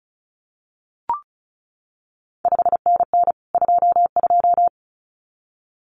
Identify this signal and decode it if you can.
Morse code, heard as a keyed tone, about 36 wpm, 700 Hz. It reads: 5NN 22
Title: CQ WW DX Contest Exchange Encore - 35wpm